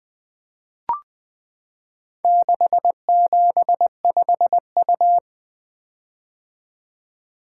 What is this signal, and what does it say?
Morse code, heard as a keyed tone, about 20 wpm, 700 Hz. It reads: 675U
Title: Four Letter-Number Combinations - 20wpm